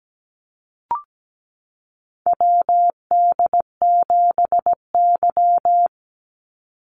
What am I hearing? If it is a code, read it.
Morse code, heard as a keyed tone, about 17 wpm, 700 Hz. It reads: WD7Y
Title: NA SOTA Chaser Callsigns 17wpm